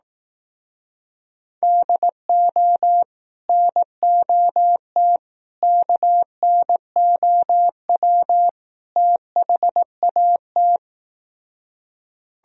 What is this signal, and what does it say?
Morse code, heard as a keyed tone, about 18 wpm, 700 Hz. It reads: DO NOT KNOW THAT
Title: Speed-Racing - Sets of 4 Words 15wpm